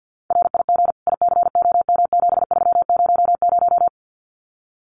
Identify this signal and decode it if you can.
Morse code, heard as a keyed tone, about 50 wpm, 700 Hz. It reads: RIG ICOM7300
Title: Common QSO Phrases 50wpm